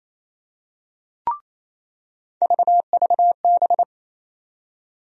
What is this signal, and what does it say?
Morse code, heard as a keyed tone, about 28 wpm, 700 Hz. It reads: VVB
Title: USA Call Sign Postfixes - 3 Letters 28wpm